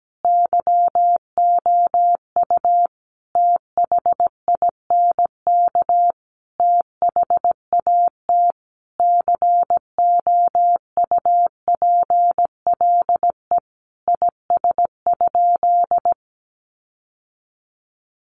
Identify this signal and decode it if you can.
Morse code, heard as a keyed tone, about 17 wpm, 700 Hz. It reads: YOU THINK THAT COUPLE IS?
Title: Sentences from Top 1100 Words 17wpm